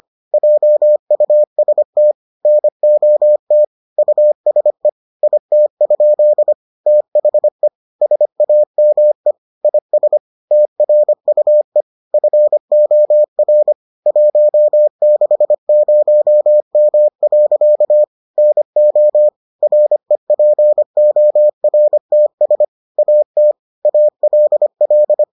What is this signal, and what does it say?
Morse code, heard as a keyed tone, about 25 wpm, 600 Hz. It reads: JUST NOT USE IT? THE SAME IS TRUE FOR 160M. NO REPORTS AT ALL